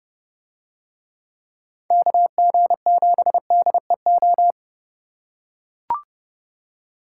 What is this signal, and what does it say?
Morse code, heard as a keyed tone, about 30 wpm, 700 Hz. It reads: KG7DEO